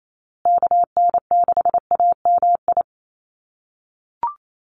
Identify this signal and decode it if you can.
Morse code, heard as a keyed tone, about 28 wpm, 690 Hz. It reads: KN6AMI
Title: USA Call Signs Repeated 3x 28wpm